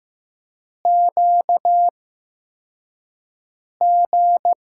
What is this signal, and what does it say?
Morse code, heard as a keyed tone, about 15 wpm, spaced as for 4 wpm, 700 Hz. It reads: QG
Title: USA Call Signs - 6 Characters - 8x Character Spacing - 15wpm